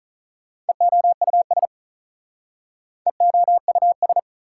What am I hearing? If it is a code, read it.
Morse code, heard as a keyed tone, about 47 wpm, 700 Hz. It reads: EOUS EOUS